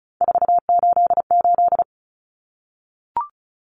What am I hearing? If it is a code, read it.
Morse code, heard as a keyed tone, about 34 wpm, 700 Hz. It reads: 488
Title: Common Cut Numbers - 3-Digit 35wpm